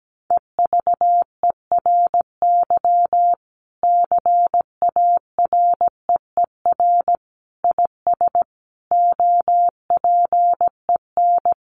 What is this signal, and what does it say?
Morse code, heard as a keyed tone, about 17 wpm, 700 Hz. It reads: EVERY CAREER IS OPEN
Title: Sentences from Top 1000 Words 17wpm